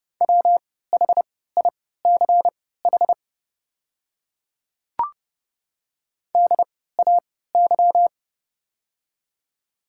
Morse code, heard as a keyed tone, about 30 wpm, spaced as for 17 wpm, 700 Hz. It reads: WHICH DAY
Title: 3x Character Spacing - Top 100 Words - 30wpm